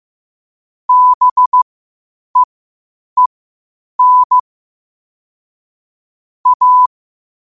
Morse code, heard as a keyed tone, about 15 wpm, spaced as for 8 wpm, 1000 Hz. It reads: BEEN A
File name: qotdfiles / 8wpm